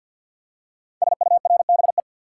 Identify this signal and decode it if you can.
Morse code, heard as a keyed tone, about 48 wpm, 690 Hz. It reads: SURBE
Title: Code Groups - 5 Letters - 50wpm